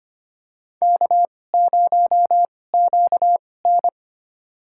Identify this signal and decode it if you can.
Morse code, heard as a keyed tone, about 25 wpm, spaced as for 18 wpm, 700 Hz. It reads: K0QN
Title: USA Call Signs - 2x Character Spacing - 25wpm